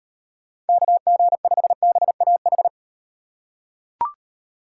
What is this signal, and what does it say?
Morse code, heard as a keyed tone, about 35 wpm, 700 Hz. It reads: KG5BAH